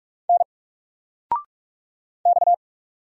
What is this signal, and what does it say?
Morse code, heard as a keyed tone, about 44 wpm, 700 Hz. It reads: N X
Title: Single Character (Letter:Number:Punctuation:Prosign) 45wpm